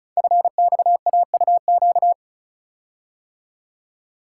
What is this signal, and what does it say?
Morse code, heard as a keyed tone, about 36 wpm, 700 Hz. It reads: FXAUQ
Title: Code Groups - 5 Letters - 35wpm